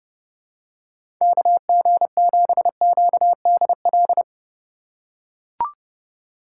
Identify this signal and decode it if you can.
Morse code, heard as a keyed tone, about 32 wpm, 700 Hz. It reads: KG7QDL
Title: USA Call Signs - 6 Characters - 2x Character Spacing - 30wpm